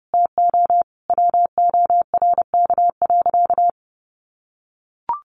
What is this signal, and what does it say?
Morse code, heard as a keyed tone, about 30 wpm, 700 Hz. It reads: TO WORK.
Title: Sentences from Top 600 Words 30wpm